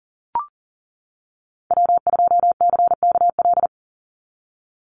Morse code, heard as a keyed tone, about 42 wpm, 700 Hz. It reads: W2CKL
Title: NA SOTA Activator Callsigns 40wpm